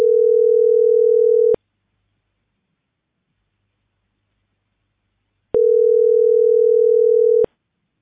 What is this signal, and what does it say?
Ringback tone (North America), 440 + 480 Hz.